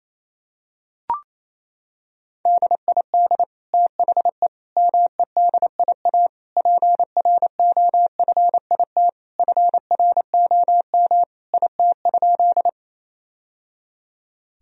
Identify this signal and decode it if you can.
Morse code, heard as a keyed tone, about 27 wpm, 700 Hz. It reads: DID THE MEDIA PROFIT FROM IT?
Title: Sentences from Top 1000 Words - Review Part 4 - 28wpm